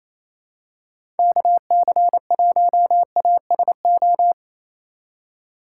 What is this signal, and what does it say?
Morse code, heard as a keyed tone, about 28 wpm, 700 Hz. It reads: KC1ASO
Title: USA Call Signs - 4x Character Spacing - 28wpm